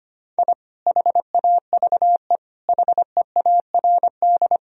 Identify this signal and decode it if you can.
Morse code, heard as a keyed tone, about 25 wpm, 700 Hz. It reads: I HAVE HEARD
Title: Sentences from Top 1000 Words - Review Part 3 - 25wpm